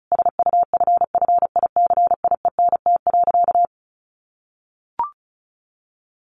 Morse code, heard as a keyed tone, about 34 wpm, 700 Hz. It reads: SUFFICIENT.
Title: Sentences from Top 1400 Words 35wpm